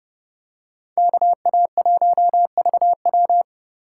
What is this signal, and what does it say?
Morse code, heard as a keyed tone, about 29 wpm, 700 Hz. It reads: KA1VW